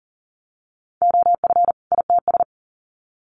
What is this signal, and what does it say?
Morse code, heard as a keyed tone, about 40 wpm, 700 Hz. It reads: OF ITS
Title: Sets of 2 Words 40wpm